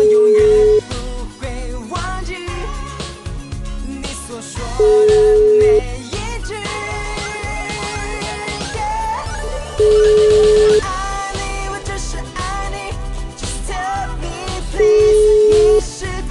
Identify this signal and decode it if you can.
Ringback tone (Europe, 425 Hz).